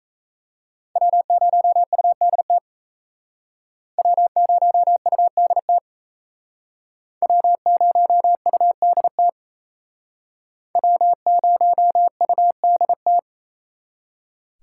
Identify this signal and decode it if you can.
Morse code, heard as a keyed tone, about 37 wpm, 700 Hz. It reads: W0UDT W0UDT W0UDT W0UDT